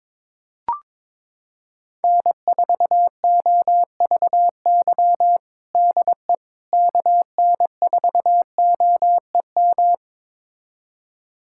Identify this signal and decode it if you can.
Morse code, heard as a keyed tone, about 22 wpm, 700 Hz. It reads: N4OVY DE KN4OEM
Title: USA Callsign DE Callsign 22wpm